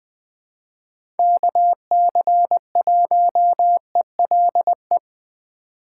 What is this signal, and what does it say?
Morse code, heard as a keyed tone, about 20 wpm, 700 Hz. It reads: KC1ELE
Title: USA Call Signs - 6 Characters - 4x Character Spacing - 20wpm